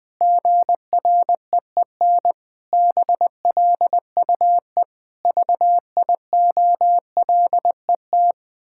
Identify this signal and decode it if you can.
Morse code, heard as a keyed tone, about 19 wpm, 700 Hz. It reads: GREEN BLUE VIOLET
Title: Pangram Challenge v2 - 20wpm